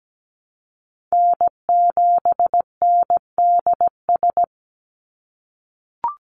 Morse code, heard as a keyed tone, about 17 wpm, 700 Hz. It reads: N7NDS
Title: USA Call Signs 17wpm